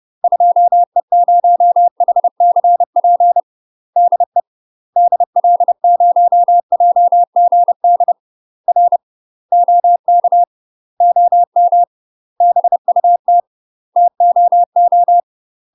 Morse code, heard as a keyed tone, about 30 wpm, 700 Hz. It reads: 2E0HCP DE DL0JGD R OK OM BUT TOO
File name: UK MASTER QSOs TRAINING / 30WPM / UK MASTER QSOs 0009